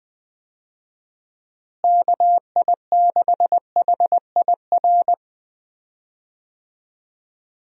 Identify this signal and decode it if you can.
Morse code, heard as a keyed tone, about 20 wpm, 700 Hz. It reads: KI6HIR
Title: USA Call Signs - 6 Characters - Repeated 3x - 20wpm